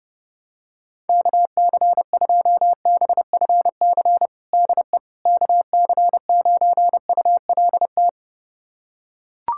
Morse code, heard as a keyed tone, about 30 wpm, 700 Hz. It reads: KC2BFC DE KC9ULT
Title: USA Callsign DE Callsign - 6 Characters - 30wpm